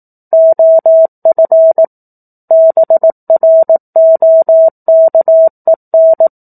Morse code, heard as a keyed tone, about 18 wpm, 650 Hz. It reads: OF BROKEN
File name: 04-Chapter-04 18 WPM